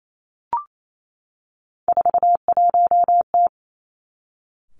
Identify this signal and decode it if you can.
Morse code, heard as a keyed tone, about 28 wpm, 700 Hz. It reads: 41T
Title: Three Challenging Letter-Number Combinations 28wpm